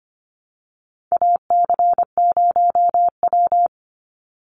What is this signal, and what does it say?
Morse code, heard as a keyed tone, about 25 wpm, 700 Hz. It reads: AC0W
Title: Call Signs Repeated 2x - 25wpm